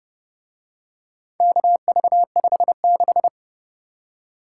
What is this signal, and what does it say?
Morse code, heard as a keyed tone, about 30 wpm, 700 Hz. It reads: KV56